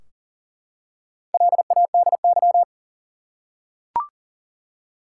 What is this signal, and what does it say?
Morse code, heard as a keyed tone, about 40 wpm, 700 Hz. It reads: LADY
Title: Four Letter Words - 40wpm